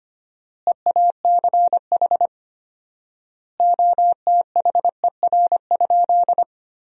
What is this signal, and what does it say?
Morse code, heard as a keyed tone, about 25 wpm, 700 Hz. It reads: EACH OTHER?
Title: 4x Word-Spacing - Sentences from Top 200 Words - 25wpm